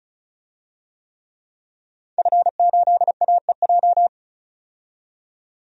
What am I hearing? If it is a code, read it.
Morse code, heard as a keyed tone, about 35 wpm, 700 Hz. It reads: F8AEJ
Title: DXCC-100 Call Signs - 5 Characters - 35wpm